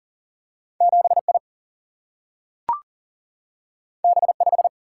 Morse code, heard as a keyed tone, about 42 wpm, 700 Hz. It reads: 7 I B 5
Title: Single-Letter Single-Number 40wpm